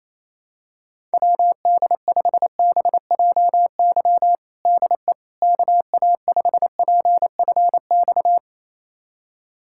Morse code, heard as a keyed tone, about 27 wpm, 700 Hz. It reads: WD5BJY DE KA5PFX
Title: USA Callsign DE Callsign - 6 Characters - 28wpm